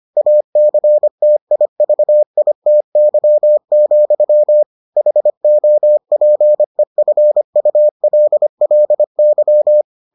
Morse code, heard as a keyed tone, about 25 wpm, 600 Hz. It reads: ACTIVITY, HOPEFULLY